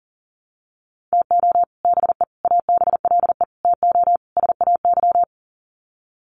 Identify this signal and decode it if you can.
Morse code, heard as a keyed tone, about 40 wpm, 700 Hz. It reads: TO BE ABLE TO SAY
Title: Sets of 5 Words Encore2 40wpm